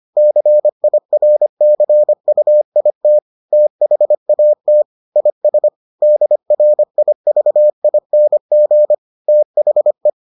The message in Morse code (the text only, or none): CIRCUIT THAT IS DRIVING THE